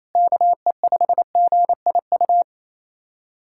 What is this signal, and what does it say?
Morse code, heard as a keyed tone, about 28 wpm, 700 Hz. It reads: KE5GIU
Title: USA Callsign DE Callsign - 6 Characters - 28wpm